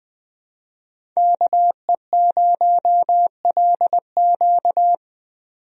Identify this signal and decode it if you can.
Morse code, heard as a keyed tone, about 20 wpm, 700 Hz. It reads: KE0LQ
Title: USA Call Signs - 5 Characters - 20wpm